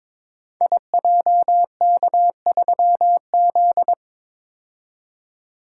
Morse code, heard as a keyed tone, about 22 wpm, 700 Hz. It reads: IJK3Z
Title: Code Groups - 5 Letter:Numbers - 22wpm